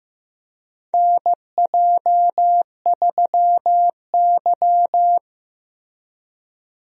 Morse code, heard as a keyed tone, about 15 wpm, 700 Hz. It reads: NJ3Y
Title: USA Call Signs - 4 Characters - Repeated 3x - 15wpm